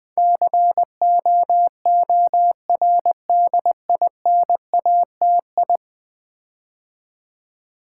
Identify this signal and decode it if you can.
Morse code, heard as a keyed tone, about 20 wpm, 700 Hz. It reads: COORDINATI
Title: Build A Word - 5 Letters At A Time - 12-Letter Words - 20wpm